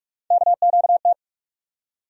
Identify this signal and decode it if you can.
Morse code, heard as a keyed tone, about 46 wpm, 700 Hz. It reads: KQT
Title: USA Call Sign Postfixes - 3 Letters 45wpm